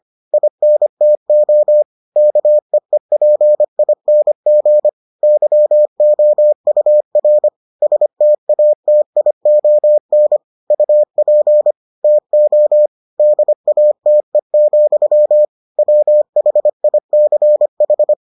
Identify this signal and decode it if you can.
Morse code, heard as a keyed tone, about 25 wpm, 600 Hz. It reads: INTO KEEPING YOUR STATION UP TO DATE, WHICH